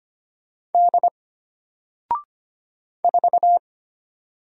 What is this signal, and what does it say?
Morse code, heard as a keyed tone, about 25 wpm, 700 Hz. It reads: D 4
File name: Single-Letter-Number-25wpm